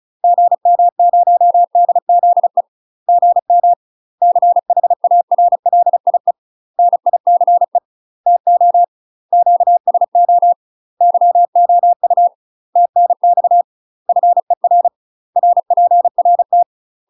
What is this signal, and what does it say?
Morse code, heard as a keyed tone, about 35 wpm, 700 Hz. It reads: GM0DZE GM CHARLIE NICE TO QSO YOU TNX FER RPRT
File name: UK MASTER QSOs TRAINING / 35WPM / UK MASTER QSOs 0041